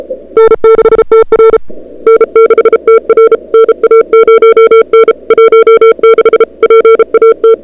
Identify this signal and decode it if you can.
Morse code, heard as a keyed tone, about 37 wpm, 440 Hz. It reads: N6TR N6TR NA0N 16 PAT
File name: n6tr / Fab2007CWSprintHour1